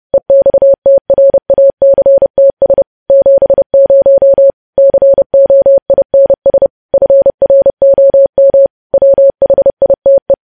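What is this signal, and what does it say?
Morse code, heard as a keyed tone, about 30 wpm, 550 Hz. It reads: EXTRACTS 70 COINS FROM WHITE
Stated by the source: News headlines sent as Morse practice at 30.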